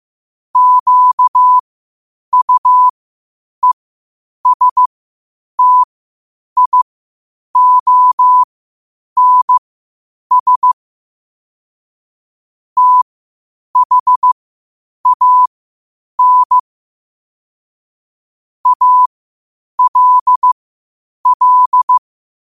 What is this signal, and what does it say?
Morse code, heard as a keyed tone, about 15 wpm, spaced as for 8 wpm, 1000 Hz. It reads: QUESTIONS THAN ALL
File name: qotdfiles / 8wpm